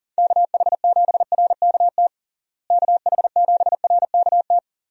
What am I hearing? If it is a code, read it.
Morse code, heard as a keyed tone, about 40 wpm, 700 Hz. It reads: KH7RKT KH7RKT
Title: USA Call Signs - 6 Characters - Repeated 3x - 40wpm